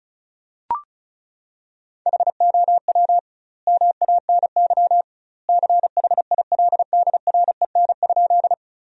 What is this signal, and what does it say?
Morse code, heard as a keyed tone, about 36 wpm, 700 Hz. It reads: HOW MANY CHILDREN?
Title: 2x Word-Spacing - Sentences from Top 300 Words - 35wpm